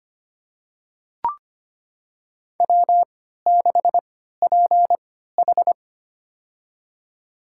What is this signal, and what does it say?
Morse code, heard as a keyed tone, about 25 wpm, spaced as for 14 wpm, 700 Hz. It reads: W6PH
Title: USA Call Signs - 4 Characters - 3x Character Spacing - 25wpm